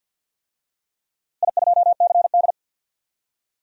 Morse code, heard as a keyed tone, about 47 wpm, 700 Hz. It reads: I2CD
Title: DXCC-100 Call Signs - 4 Characters - 50wpm